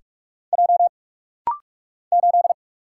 Morse code, heard as a keyed tone, about 46 wpm, 700 Hz. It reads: J 8
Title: Single Letter-Number - Rapid-Fire 45wpm